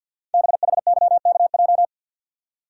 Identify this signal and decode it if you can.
Morse code, heard as a keyed tone, about 52 wpm, 700 Hz. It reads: BHYKJ
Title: Code Groups - 5 Letters - 50wpm